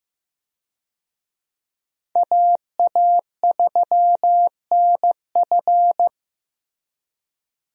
Morse code, heard as a keyed tone, about 15 wpm, 700 Hz. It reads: AA3NF
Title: USA Call Signs - 5 Characters - Repeated 3x - 15wpm